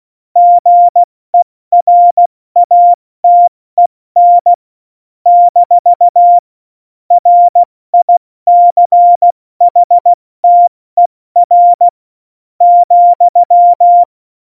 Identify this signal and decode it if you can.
Morse code, heard as a keyed tone, about 16 wpm, spaced as for 13 wpm, 700 Hz. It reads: GERATEN - RICHTER,
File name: Morsekurs / Witze / Witz 05. Januar 2021, 70 F80